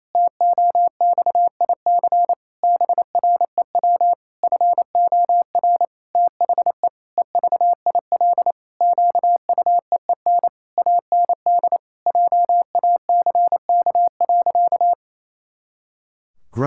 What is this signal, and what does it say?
Morse code, heard as a keyed tone, about 28 wpm, 700 Hz. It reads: TOXIC BREW FOR THE EVIL QUEEN AND JACK.
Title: Pangram Challenge v2 - 28wpm